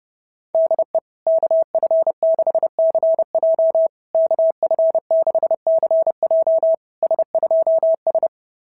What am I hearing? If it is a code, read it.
Morse code, heard as a keyed tone, about 29 wpm, 660 Hz. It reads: DE KF6CJ KF6CJ S2S
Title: SOTA QSO Practice 30wpm